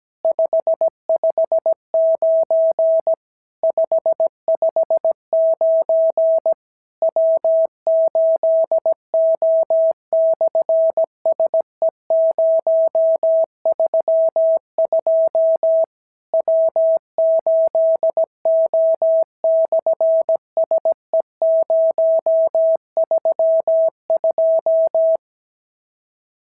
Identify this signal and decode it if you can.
Morse code, heard as a keyed tone, about 17 wpm, 650 Hz. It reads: 559 559 W8O/SE032 W8O/SE032
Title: SOTA QSO Practice 17wpm